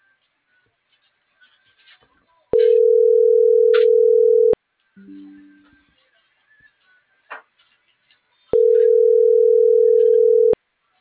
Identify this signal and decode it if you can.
Ringback tone (North America), 440 + 480 Hz.